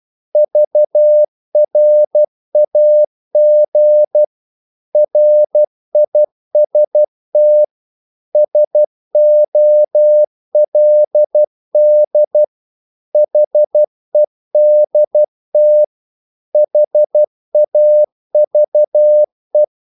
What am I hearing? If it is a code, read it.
Morse code, heard as a keyed tone, about 12 wpm, 600 Hz. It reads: VRAG RIST SOLD HEDT HAVE
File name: Korte ord DK 12wpm